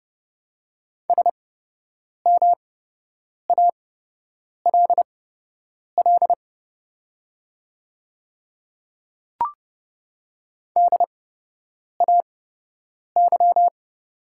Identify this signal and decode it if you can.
Morse code, heard as a keyed tone, about 30 wpm, spaced as for 8 wpm, 700 Hz. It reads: SMALL DAY
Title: 8x Character Spacing - Top 200 Words - 30wpm